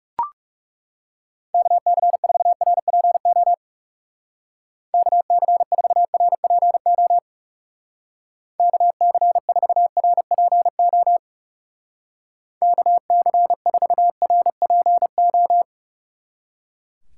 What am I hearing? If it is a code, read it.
Morse code, heard as a keyed tone, about 35 wpm, 700 Hz. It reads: KC4RPO KC4RPO KC4RPO KC4RPO
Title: Speed Racing - USA 2x3 Call Signs 30wpm